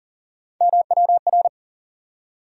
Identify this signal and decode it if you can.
Morse code, heard as a keyed tone, about 40 wpm, 700 Hz. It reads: MWR
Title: USA Call Sign Postfixes - 3 Letters 40wpm